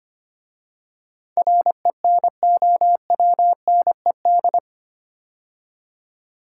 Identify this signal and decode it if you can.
Morse code, heard as a keyed tone, about 25 wpm, 700 Hz. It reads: RENOWNED
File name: 8 Letter Words - First 3 Letters Given - 25wpm